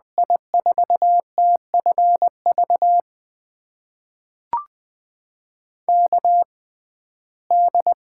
Morse code, heard as a keyed tone, about 21 wpm, spaced as for 15 wpm, 700 Hz. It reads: I4TFV KD
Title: USA Call Signs - 6 Characters - 6x Character Spacing - 20wpm